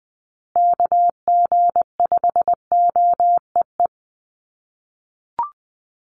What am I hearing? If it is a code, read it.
Morse code, heard as a keyed tone, about 20 wpm, 700 Hz. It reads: KG5OEE
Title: USA Call Signs Repeated 2x 20wpm